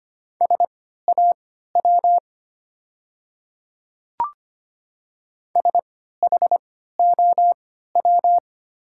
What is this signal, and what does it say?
Morse code, heard as a keyed tone, about 25 wpm, spaced as for 14 wpm, 690 Hz. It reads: SAW SHOW
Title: 3x Character Spacing - Top 300 Words - 25wpm